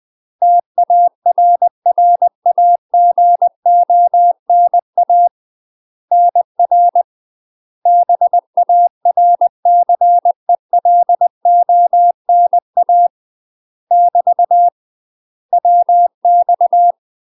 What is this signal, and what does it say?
Morse code, heard as a keyed tone, about 20 wpm, 700 Hz. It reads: TARRAGONA NR BARCELONA <BT> WX
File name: UK MASTER QSOs TRAINING / 20WPM / UK MASTER QSOs 0053